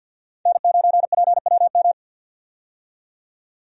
Morse code, heard as a keyed tone, about 51 wpm, 700 Hz. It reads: N9PWM
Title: USA Call Signs - 5 Characters - 50wpm